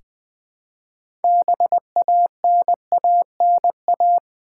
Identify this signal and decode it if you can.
Morse code, heard as a keyed tone, about 20 wpm, 700 Hz. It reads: BANANA
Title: 6 Letter Words - First 3 Letters Given - 20wpm